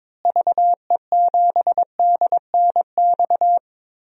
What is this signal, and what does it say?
Morse code, heard as a keyed tone, about 22 wpm, 700 Hz. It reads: VE7DNX
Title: DXCC-100 Call Signs - 6 Characters - Repeated 2x - 22wpm